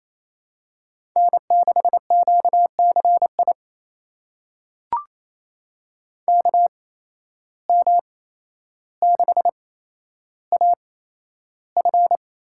Morse code, heard as a keyed tone, about 28 wpm, spaced as for 12 wpm, 700 Hz. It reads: N6QCI KM6AF
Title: USA Call Signs - 5 Characters - 8x Character Spacing - 28wpm